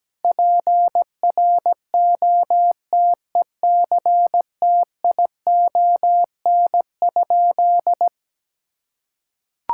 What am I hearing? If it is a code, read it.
Morse code, heard as a keyed tone, about 17 wpm, 700 Hz. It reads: PROTECTION?
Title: Sentences from Top 1600 Words 17wpm